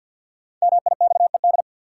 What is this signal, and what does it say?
Morse code, heard as a keyed tone, about 53 wpm, 700 Hz. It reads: MIXED
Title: Sentences from Top 1500 Words - New Word Drill - 50wpm